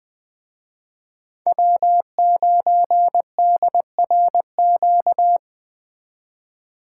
Morse code, heard as a keyed tone, about 19 wpm, 700 Hz. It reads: W9DRQ
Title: Speed Racing - USA 1x3 Call Signs 17wpm